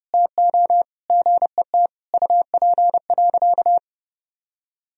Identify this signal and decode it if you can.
Morse code, heard as a keyed tone, about 30 wpm, 700 Hz. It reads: TO GET UP.
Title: Sentences from Top 500 Words and Abbreviations 30wpm